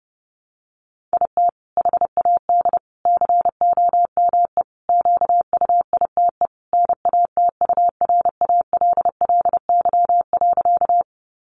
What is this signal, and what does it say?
Morse code, heard as a keyed tone, about 30 wpm, 700 Hz. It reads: IT HAD COME QUITE NATURALLY.
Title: Sentences from Top 1300 Words 30wpm